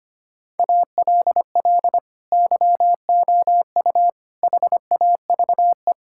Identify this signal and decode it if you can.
Morse code, heard as a keyed tone, about 25 wpm, 700 Hz. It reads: ALL YOU HAVE
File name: practice / 25wpm / Sentences with 3-Word Phrases - 25wpm